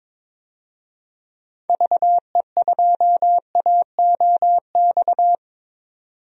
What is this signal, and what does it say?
Morse code, heard as a keyed tone, about 22 wpm, 700 Hz. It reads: VE2AOX
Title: DXCC-100 Call Signs - 6 Characters - 22wpm